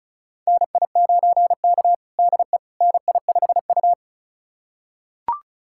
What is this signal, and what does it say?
Morse code, heard as a keyed tone, about 36 wpm, 700 Hz. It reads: NI9K DE NI5U
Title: USA Callsign DE Callsign - 4 Characters - 35wpm